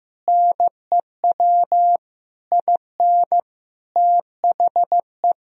Morse code, heard as a keyed tone, about 14 wpm, 700 Hz. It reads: NEW IN THE
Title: Sentences from Top 1000 Words - Review Part 3 - 15wpm